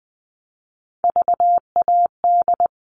Morse code, heard as a keyed tone, about 20 wpm, 700 Hz. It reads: VAD
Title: All Cut Numbers - 3-Digit 20wpm